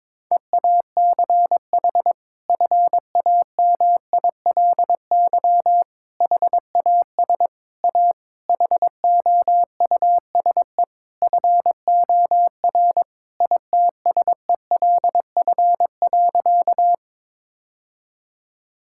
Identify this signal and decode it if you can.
Morse code, heard as a keyed tone, about 22 wpm, 700 Hz. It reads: EACH FAMILY HAS A HOUSE FOR ITSELF.
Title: Sentences from Top 1000 Words - Review Part 5 - 22wpm